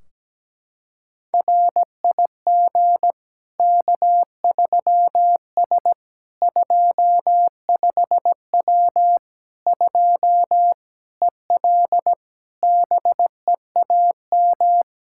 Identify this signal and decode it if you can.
Morse code, heard as a keyed tone, about 17 wpm, 700 Hz. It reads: RIG K3S 25W 2 EL BEAM
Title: Practice QSO - Exchange 2 - Info - 17wpm